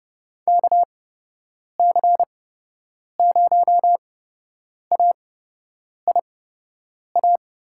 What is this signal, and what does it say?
Morse code, heard as a keyed tone, about 30 wpm, spaced as for 8 wpm, 700 Hz. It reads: KC0AIA
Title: USA Call Signs - 6 Characters - 8x Character Spacing - 30wpm